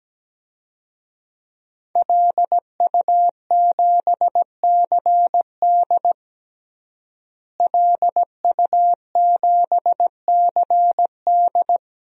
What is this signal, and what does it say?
Morse code, heard as a keyed tone, about 17 wpm, 700 Hz. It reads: LU7CD LU7CD
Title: DXCC-100 Call Signs - 5 Characters - Repeated 3x - 17wpm